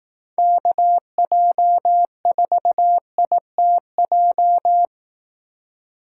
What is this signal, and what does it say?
Morse code, heard as a keyed tone, about 18 wpm, 700 Hz. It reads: KJ4ITJ
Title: Speed Racing - USA 2x3 Call Signs 15wpm